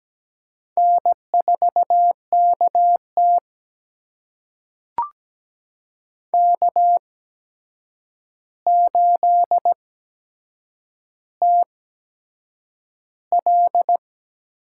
Morse code, heard as a keyed tone, about 17 wpm, spaced as for 7 wpm, 700 Hz. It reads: N4KT K8TL
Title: USA Call Signs - 4 Characters - 8x Character Spacing - 17wpm